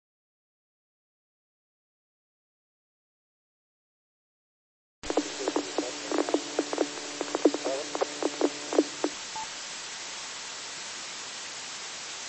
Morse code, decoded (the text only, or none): none